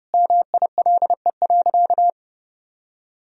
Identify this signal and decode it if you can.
Morse code, heard as a keyed tone, about 32 wpm, 700 Hz. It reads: MILE.
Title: Sentences from Top 1000 Words - Review Part 1 - 30wpm